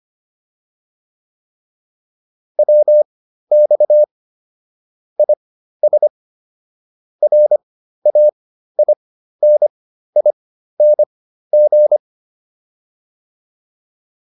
Morse code, heard as a keyed tone, about 25 wpm, spaced as for 13 wpm, 600 Hz. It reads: WX IS RAINING